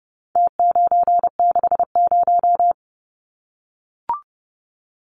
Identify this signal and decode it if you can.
Morse code, heard as a keyed tone, about 30 wpm, 700 Hz. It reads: T960
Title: Numbers - Contest Style - 4-Digit 30wpm